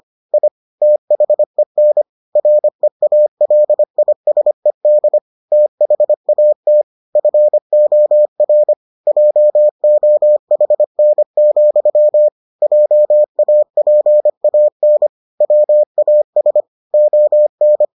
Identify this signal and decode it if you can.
Morse code, heard as a keyed tone, about 25 wpm, 600 Hz. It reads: I THEN REALISED THAT FOR JOHN, JAPAN WAS ON